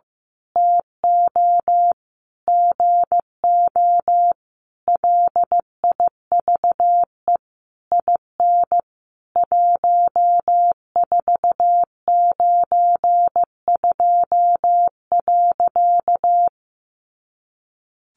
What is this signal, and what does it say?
Morse code, heard as a keyed tone, about 15 wpm, 700 Hz. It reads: TO GO LIVE IN 1492.
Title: A Sound Of Thunder 15wpm